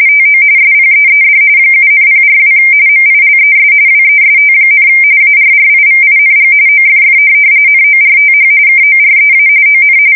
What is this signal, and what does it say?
RTTY, about 45.45 baud. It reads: TO SPEED 7,042 MILES WEST, ARRIVING OFF NORTH KOREA ON
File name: fsk170 / USS-New-Jersey